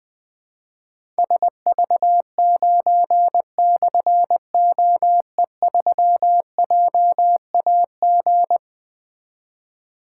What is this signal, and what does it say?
Morse code, heard as a keyed tone, about 20 wpm, 700 Hz. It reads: SV9/OE3JAG
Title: Difficult Call Signs 20wpm